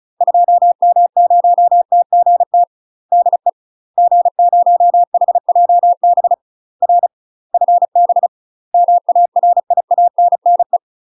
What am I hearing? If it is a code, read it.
Morse code, heard as a keyed tone, about 34 wpm, 700 Hz. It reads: 2M0TGT DE G0HJB R FB MARIANNE
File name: UK MASTER QSOs TRAINING / 35WPM / UK MASTER QSOs 0034